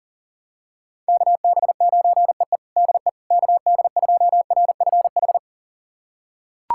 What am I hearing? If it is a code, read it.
Morse code, heard as a keyed tone, about 37 wpm, 700 Hz. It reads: KB9EE DE KD2RFH
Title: USA Callsign DE Callsign 40wpm